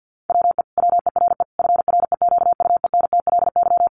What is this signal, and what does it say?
Morse code, heard as a keyed tone, about 50 wpm, 700 Hz. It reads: WE WERE FREQUENTLY
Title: Sentences from Top 1300 Words 50wpm